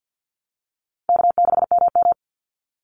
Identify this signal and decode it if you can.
Morse code, heard as a keyed tone, about 49 wpm, 700 Hz. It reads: K6MM
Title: Call Signs Repeated 3x - 50wpm